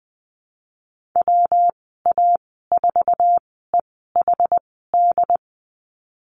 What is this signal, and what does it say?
Morse code, heard as a keyed tone, about 20 wpm, spaced as for 15 wpm, 700 Hz. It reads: WA4EHD
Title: 179 Lesson 29g - Slash - Call Signs 20wpm